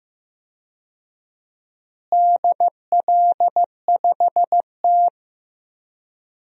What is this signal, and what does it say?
Morse code, heard as a keyed tone, about 15 wpm, 700 Hz. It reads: DL5T
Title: DXCC-100 Call Signs - 4 Characters - Repeated 3x - 15wpm